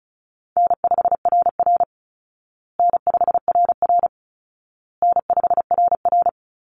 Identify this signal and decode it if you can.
Morse code, heard as a keyed tone, about 34 wpm, 700 Hz. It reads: N5RR N5RR N5RR
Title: Call Signs Repeated 2x - 35wpm